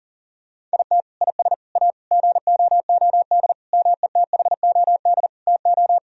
Morse code, heard as a keyed tone, about 40 wpm, 700 Hz. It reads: IT IS A GOOD METHOD TO
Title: Sentences from Top 1000 Words - Review Part 3 - 40wpm